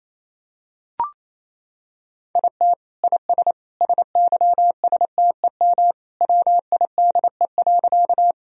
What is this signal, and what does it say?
Morse code, heard as a keyed tone, about 28 wpm, 700 Hz. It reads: IT IS SYSTEM WIDE.
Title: Sentences from Top 1000 Words - Review Part 3 - 28wpm